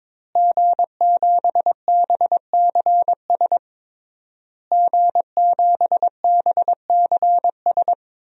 Morse code, heard as a keyed tone, about 22 wpm, 700 Hz. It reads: G7BCS G7BCS
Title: DXCC-100 Call Signs - 5 Characters - Repeated 3x - 22wpm